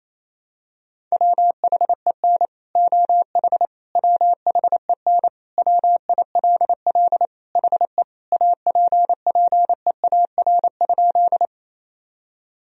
Morse code, heard as a keyed tone, about 28 wpm, 700 Hz. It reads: WHEN OH WHEN WILL HE APPEAR?
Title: Sentences from Top 1000 Words - Review Part 2 - 28wpm